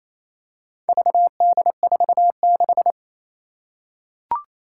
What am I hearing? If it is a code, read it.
Morse code, heard as a keyed tone, about 29 wpm, 700 Hz. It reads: VD46
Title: Four Challenging Letter-Number Combinations 28wpm